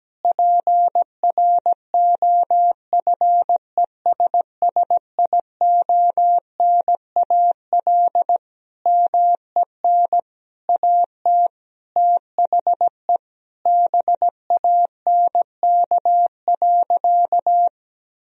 Morse code, heard as a keyed tone, about 17 wpm, 700 Hz. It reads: PROFESSIONAL MEN AT THE BANK.
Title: Sentences from Top 1000 Words - Review Part 4 - 17wpm